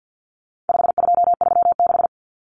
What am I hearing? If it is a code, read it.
Morse code, heard as a keyed tone, about 50 wpm, 700 Hz. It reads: 5236
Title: Numbers - 4-Digit 50wpm